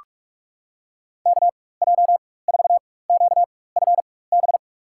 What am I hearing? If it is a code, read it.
Morse code, heard as a keyed tone, about 46 wpm, spaced as for 21 wpm, 700 Hz. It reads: KJ4QFB
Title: USA Call Signs - 6 Characters - 4x Character Spacing - 45wpm